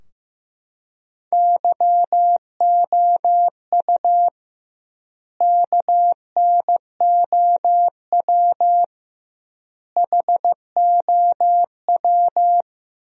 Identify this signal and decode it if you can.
Morse code, heard as a keyed tone, about 16 wpm, 700 Hz. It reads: YOU KNOW HOW
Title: 2x Word-Spacing - Sentences from Top 100 Words - 15wpm